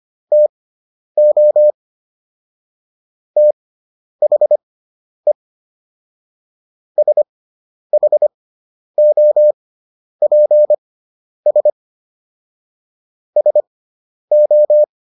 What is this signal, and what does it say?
Morse code, heard as a keyed tone, about 25 wpm, spaced as for 10 wpm, 600 Hz. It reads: TO THE SHOPS SO